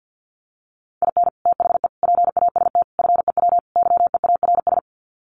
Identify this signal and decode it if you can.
Morse code, heard as a keyed tone, about 50 wpm, 700 Hz. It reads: IN THE PAST FEW YEARS
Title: Sets of 5 Words 50wpm